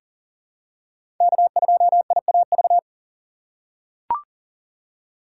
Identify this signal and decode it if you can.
Morse code, heard as a keyed tone, about 37 wpm, 700 Hz. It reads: K2IAV
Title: USA Call Signs - 6x Character Spacing - 40wpm